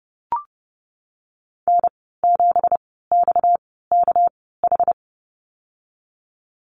Morse code, heard as a keyed tone, about 30 wpm, spaced as for 17 wpm, 700 Hz. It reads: N7XKH
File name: Lesson 37g - Letter x - Call Signs 30wpm